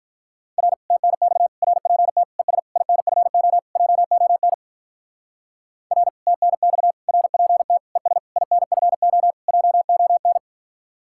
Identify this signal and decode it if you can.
Morse code, heard as a keyed tone, about 42 wpm, 700 Hz. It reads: R TNX RPT ES INFO JON R TNX RPT ES INFO JON
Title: Speed-Racing Common QSO Phrases - 35wpm